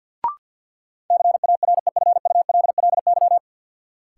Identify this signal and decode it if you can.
Morse code, heard as a keyed tone, about 53 wpm, 700 Hz. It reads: CAREFULLY
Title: Sentences from Top 1300 Words - New Word Drill - 50wpm